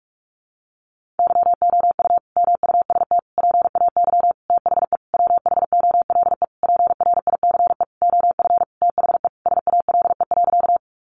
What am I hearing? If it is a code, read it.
Morse code, heard as a keyed tone, about 44 wpm, 700 Hz. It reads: YOU MUST PAY THE WHOLE PRICE OF THE SALE.
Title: Sentences from Top 1400 Words 45wpm